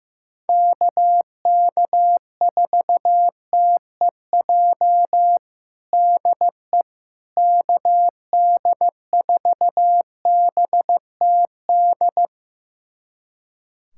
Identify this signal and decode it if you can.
Morse code, heard as a keyed tone, about 15 wpm, 700 Hz. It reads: KK4TEJ DE KD4BTD
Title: USA Callsign DE Callsign - 6 Characters - 15wpm